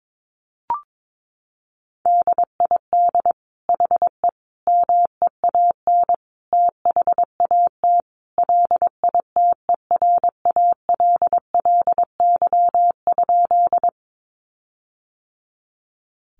Morse code, heard as a keyed tone, about 22 wpm, 700 Hz. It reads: DID HE MEAN THAT LITERALLY?
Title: Sentences from Top 1400 Words 22wpm